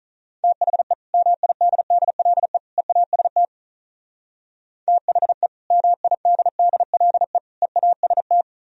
Morse code, heard as a keyed tone, about 44 wpm, 700 Hz. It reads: THE MIDDLE EAST THE MIDDLE EAST